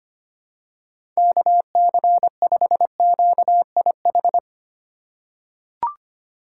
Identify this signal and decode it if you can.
Morse code, heard as a keyed tone, about 25 wpm, 690 Hz. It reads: KC5QIH
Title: USA Call Signs - 4x Character Spacing - 25wpm